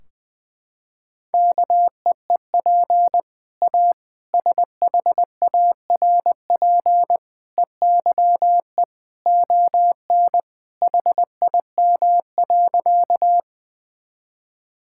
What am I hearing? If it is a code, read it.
Morse code, heard as a keyed tone, about 20 wpm, 700 Hz. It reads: KEEP A SHARP EYE ON HIM.
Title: Sentences from Top 1600 Words 20wpm